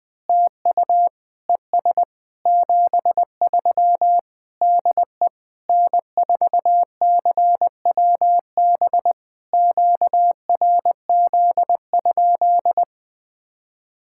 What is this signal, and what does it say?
Morse code, heard as a keyed tone, about 20 wpm, 690 Hz. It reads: TU ES 73 DE N4CWB QRZ?
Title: POTA QSO Practice - 20wpm